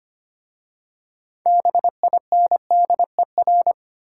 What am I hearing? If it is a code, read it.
Morse code, heard as a keyed tone, about 25 wpm, 700 Hz. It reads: BINDER
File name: practice / 25wpm / Difficult Words with Hints - 25wpm